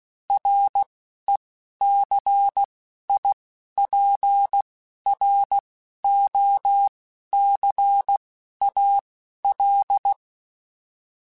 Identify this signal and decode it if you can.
Morse code, heard as a keyed tone, about 16 wpm, spaced as for 11 wpm, 800 Hz. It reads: RECIPROCAL
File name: Technical Subjects / Technical Subjects-mp3s / XC @10